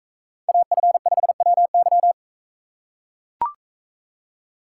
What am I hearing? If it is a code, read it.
Morse code, heard as a keyed tone, about 42 wpm, 700 Hz. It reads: AF5WY